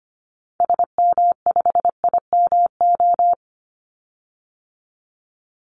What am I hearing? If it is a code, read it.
Morse code, heard as a keyed tone, about 25 wpm, 700 Hz. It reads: SM5IMO
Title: Call Signs Repeated 3x - 25wpm